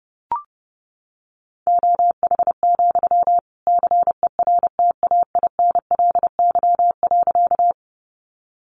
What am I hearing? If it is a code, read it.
Morse code, heard as a keyed tone, about 30 wpm, 700 Hz. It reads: OH, CERTAINLY.
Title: Sentences from Top 900 Words 30wpm